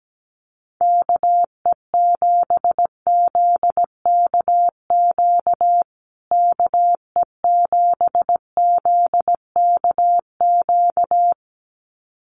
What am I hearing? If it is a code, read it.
Morse code, heard as a keyed tone, about 17 wpm, 700 Hz. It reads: KE7ZKQ KE7ZKQ
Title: USA Call Signs Repeated 2x 17wpm